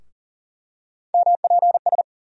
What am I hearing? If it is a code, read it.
Morse code, heard as a keyed tone, about 40 wpm, 700 Hz. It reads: MPS
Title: USA Call Sign Postfixes - 3 Letters 40wpm